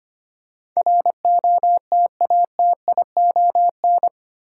Morse code, heard as a keyed tone, about 25 wpm, 700 Hz. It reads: ROTATION
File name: 8 Letter Words - First 3 Letters Given - 25wpm